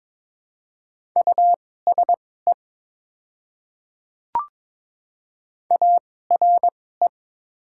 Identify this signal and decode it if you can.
Morse code, heard as a keyed tone, about 22 wpm, spaced as for 16 wpm, 700 Hz. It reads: USE ARE
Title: 2x Character Spacing - Top 100 Words - 22wpm